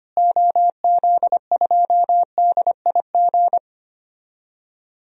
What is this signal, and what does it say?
Morse code, heard as a keyed tone, about 25 wpm, 700 Hz. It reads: OZ2DIG
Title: DXCC-100 Call Signs - 6 Characters - Repeated 2x - 25wpm